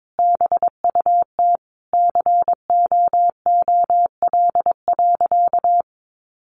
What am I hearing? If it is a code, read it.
Morse code, heard as a keyed tone, about 22 wpm, 700 Hz. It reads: BUT COOL.
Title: Sentences from Top 1200 Words 22wpm